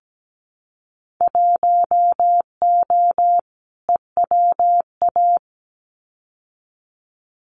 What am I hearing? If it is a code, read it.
Morse code, heard as a keyed tone, about 17 wpm, 700 Hz. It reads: 1O EWA
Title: Winter Field Day Exchanges Encore 17wpm